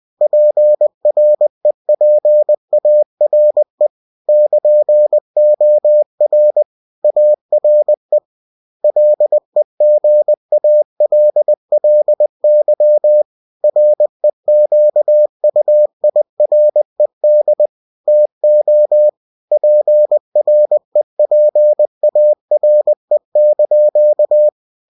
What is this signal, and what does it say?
Morse code, heard as a keyed tone, about 20 wpm, 600 Hz. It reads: PREPARE <KN>OR ARE LEGALLY REQUIRED TO PREPARE)
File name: cw-books / winnie-the-pooh / winnie-20wpm-page-0117